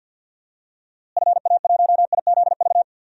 Morse code, heard as a keyed tone, about 50 wpm, 700 Hz. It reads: FA1IZV
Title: DXCC-100 Call Signs - 6 Characters - Repeated 3x - 50wpm